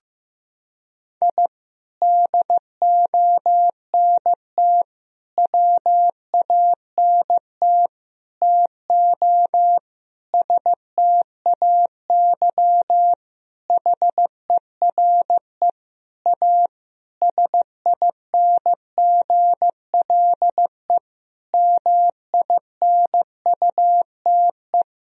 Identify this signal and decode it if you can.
Morse code, heard as a keyed tone, about 15 wpm, 700 Hz. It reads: I DONT WANT TO STAY HERE A SINGLE MINUTE
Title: Sentences from Top 1000 Words - Review Part 4 - 15wpm